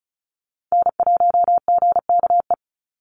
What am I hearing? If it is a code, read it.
Morse code, heard as a keyed tone, about 34 wpm, 700 Hz. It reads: N1GKE
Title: Call Signs Repeated 3x - 35wpm